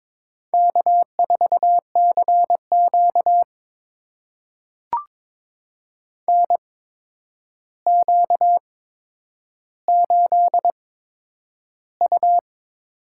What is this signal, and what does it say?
Morse code, heard as a keyed tone, about 22 wpm, spaced as for 9 wpm, 700 Hz. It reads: K4CQ NQ8U
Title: USA Call Signs - 4 Characters - 8x Character Spacing - 22wpm